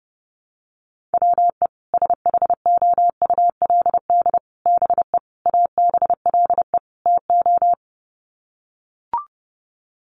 Morse code, heard as a keyed tone, about 28 wpm, 700 Hz. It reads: WE SHOULD BE ABLE TO
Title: Sets of 5 Words Encore2 30wpm